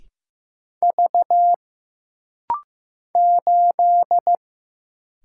Morse code, heard as a keyed tone, about 15 wpm, 700 Hz. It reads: V 8
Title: Single Letter-Number Alt - Rapid-Fire 15wpm